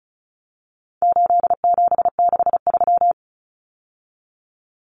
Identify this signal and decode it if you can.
Morse code, heard as a keyed tone, about 34 wpm, 700 Hz. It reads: 8763
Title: Common Cut Numbers - 4-Digit 35wpm